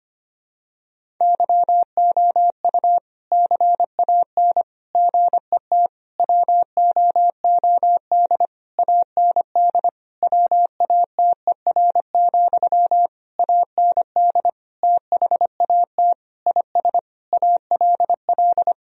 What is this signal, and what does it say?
Morse code, heard as a keyed tone, about 25 wpm, 700 Hz. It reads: YOU CAN GET WOOD AND WATER, AND THAT IS ALL
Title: Sentences from Top 1000 Words - Review Part 2 - 25wpm